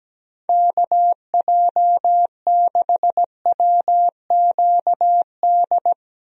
Morse code, heard as a keyed tone, about 17 wpm, 700 Hz. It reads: KJ6WQD
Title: USA Call Signs - 6 Characters - 17wpm